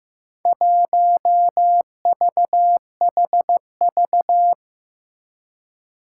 Morse code, heard as a keyed tone, about 15 wpm, 700 Hz. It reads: 1VHV
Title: Four Challenging Letter-Number Combinations 15wpm